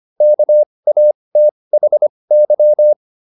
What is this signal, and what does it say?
Morse code, heard as a keyed tone, about 25 wpm, spaced as for 20 wpm, 600 Hz. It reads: KATHY